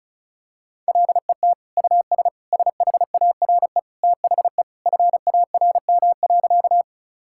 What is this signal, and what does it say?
Morse code, heard as a keyed tone, about 36 wpm, 700 Hz. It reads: LET US SHARE THE FARM.
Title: Sentences from Top 1000 Words - Review Part 6 - 35wpm